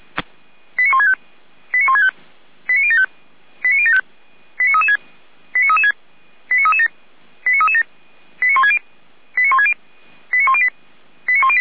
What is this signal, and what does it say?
Five-tone selective call (ZVEI), heard as quick runs of five tones, tones of about 70 ms. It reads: 89165 89165 89075 89075 89336 89336 89337 89337 89160 89160 89108 89108